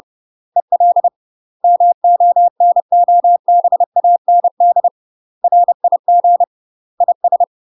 Morse code, heard as a keyed tone, about 30 wpm, 700 Hz. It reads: EL MONOBAND RIG IS
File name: UK MASTER QSOs TRAINING / 30WPM / UK MASTER QSOs 0028